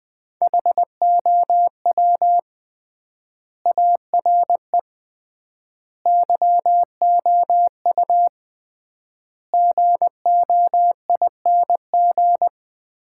Morse code, heard as a keyed tone, about 20 wpm, 700 Hz. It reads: HOW ARE YOU GOING
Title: 3x Word-Spacing - Sets of 5 Words - 20wpm